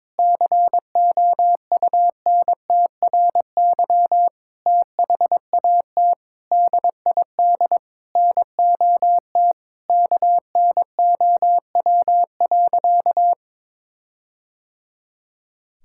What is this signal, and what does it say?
Morse code, heard as a keyed tone, about 22 wpm, 700 Hz. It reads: COUNTRY THAT DID NOT KNOW.
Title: Sentences from Top 1600 Words 22wpm